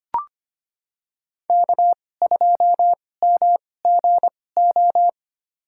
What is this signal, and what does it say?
Morse code, heard as a keyed tone, about 25 wpm, spaced as for 18 wpm, 700 Hz. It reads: K2MGO
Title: USA Call Signs - 5 Characters - 2x Character Spacing - 25wpm